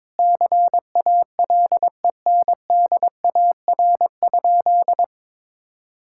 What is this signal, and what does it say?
Morse code, heard as a keyed tone, about 21 wpm, 700 Hz. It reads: CALENDAR?
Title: Sentences from Top 1600 Words 22wpm